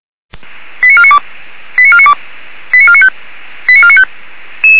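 Five-tone selective call (ZVEI), heard as quick runs of five tones, tones of about 70 ms. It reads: 89492 89492 89495 89495
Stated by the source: ZVEI five-tone call received on SDR.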